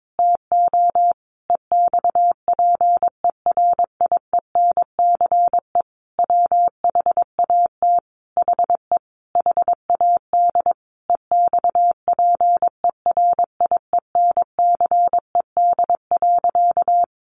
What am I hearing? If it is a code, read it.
Morse code, heard as a keyed tone, about 22 wpm, 700 Hz. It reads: TO EXPERIENCE WHAT HE HAD EXPERIENCED.
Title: Sentences from Top 1100 Words 22wpm